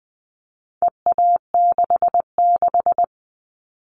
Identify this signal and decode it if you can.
Morse code, heard as a keyed tone, about 20 wpm, 700 Hz. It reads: EA66
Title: All Cut Numbers - 4-Digit 20wpm